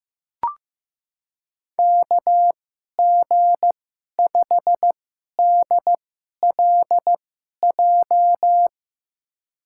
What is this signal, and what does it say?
Morse code, heard as a keyed tone, about 15 wpm, spaced as for 11 wpm, 700 Hz. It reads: KG5DLJ
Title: USA Call Signs - 6 Characters - 2x Character Spacing - 15wpm